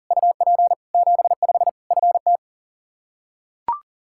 Morse code, heard as a keyed tone, about 40 wpm, 700 Hz. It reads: UP 75 FT
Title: Practice QSO - Exchange 2 - Info - 40wpm